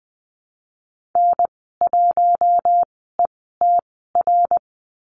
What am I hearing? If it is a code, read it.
Morse code, heard as a keyed tone, about 19 wpm, spaced as for 14 wpm, 700 Hz. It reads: N1ETR
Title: 062 Lesson 11g - Letter h - Call Signs 20wpm